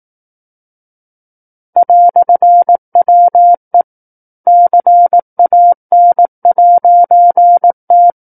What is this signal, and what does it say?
Morse code, heard as a keyed tone, about 18 wpm, 700 Hz. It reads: "WE CAN'T
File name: 18 WPM